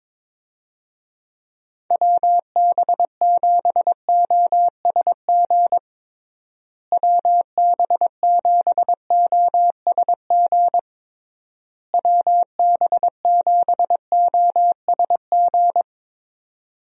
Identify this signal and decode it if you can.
Morse code, heard as a keyed tone, about 21 wpm, 700 Hz. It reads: WB7OSG WB7OSG WB7OSG
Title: USA Call Signs - 6 Characters - Repeated 3x - 22wpm